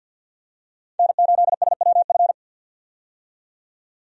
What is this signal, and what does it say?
Morse code, heard as a keyed tone, about 52 wpm, 700 Hz. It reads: N8SWF
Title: USA Call Signs - 5 Characters - 50wpm